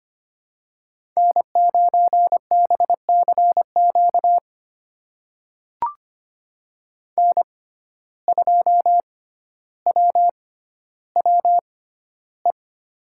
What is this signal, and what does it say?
Morse code, heard as a keyed tone, about 25 wpm, spaced as for 13 wpm, 700 Hz. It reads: N9BCQ N2WWE
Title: USA Call Signs - 5 Characters - 6x Character Spacing - 25wpm